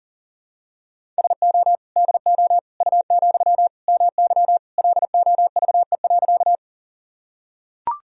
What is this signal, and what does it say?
Morse code, heard as a keyed tone, about 37 wpm, 700 Hz. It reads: SO DO U, MY LOVE.
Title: Sentences from Top 400 Words and Abbreviations 40wpm